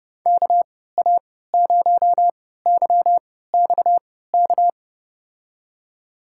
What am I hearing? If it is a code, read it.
Morse code, heard as a keyed tone, about 31 wpm, spaced as for 17 wpm, 700 Hz. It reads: KA0YXK
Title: USA Call Signs - 3x Character Spacing - 30wpm